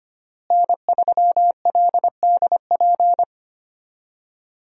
Morse code, heard as a keyed tone, about 25 wpm, 700 Hz. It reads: N3LDP
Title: USA Call Signs - 5 Characters - Repeated 3x - 25wpm